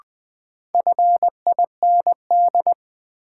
Morse code, heard as a keyed tone, about 20 wpm, 700 Hz. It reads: FIND
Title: Sentences from Top 100 Words - New Word Drill - 20wpm